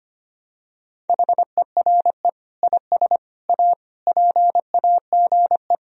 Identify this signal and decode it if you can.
Morse code, heard as a keyed tone, about 25 wpm, 700 Hz. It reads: HERE IS A PAGE
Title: Sentences from Top 1000 Words - Review Part 1 - 25wpm